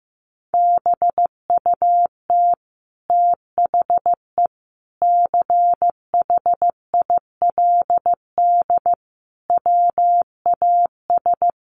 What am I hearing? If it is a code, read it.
Morse code, heard as a keyed tone, about 15 wpm, 700 Hz. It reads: BUT THE CHILD WAS
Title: Sentences from Top 1500 Words 15wpm